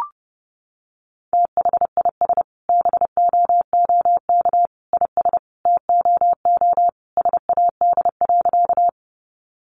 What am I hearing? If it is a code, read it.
Morse code, heard as a keyed tone, about 30 wpm, 700 Hz. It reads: THIS BOOK IS TOO SAD.
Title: Additional Easy Phrases 30wpm